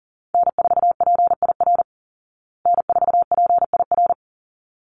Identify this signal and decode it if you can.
Morse code, heard as a keyed tone, about 40 wpm, 700 Hz. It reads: N4PIR N4PIR
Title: Call Signs Repeated 3x - 40wpm